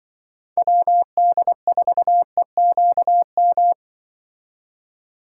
Morse code, heard as a keyed tone, about 24 wpm, 700 Hz. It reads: WD4EQM
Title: Speed Racing - USA 2x3 Call Signs 20wpm